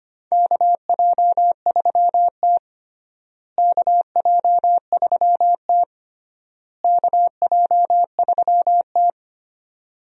Morse code, heard as a keyed tone, about 25 wpm, 700 Hz. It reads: KJ3T KJ3T KJ3T
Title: USA Call Signs - 4 Characters - Repeated 3x - 25wpm